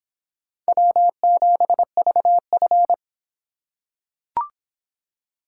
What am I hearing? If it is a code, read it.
Morse code, heard as a keyed tone, about 26 wpm, 700 Hz. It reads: W7VF